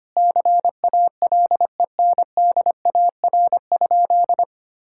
Morse code, heard as a keyed tone, about 25 wpm, 700 Hz. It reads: CALENDAR?
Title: Sentences from Top 1600 Words 25wpm